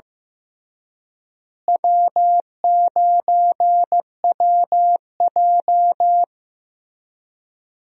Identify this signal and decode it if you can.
Morse code, heard as a keyed tone, about 16 wpm, 700 Hz. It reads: W9WJ
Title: USA Call Signs - 4 Characters - Repeated 3x - 15wpm